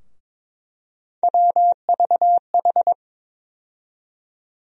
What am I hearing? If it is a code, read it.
Morse code, heard as a keyed tone, about 21 wpm, 700 Hz. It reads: WVH
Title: USA Call Sign Postfixes - 3 Letters 22wpm